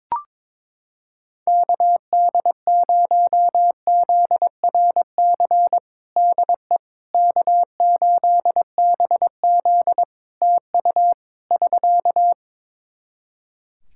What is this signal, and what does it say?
Morse code, heard as a keyed tone, about 22 wpm, 700 Hz. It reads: KD0ZRC DE K8BZ TU <SK>
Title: Practice QSO - Exchange 4 - Ending - 22wpm